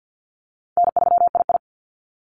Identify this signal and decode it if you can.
Morse code, heard as a keyed tone, about 48 wpm, 700 Hz. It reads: N3II
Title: NA SOTA Activator Callsigns 50wpm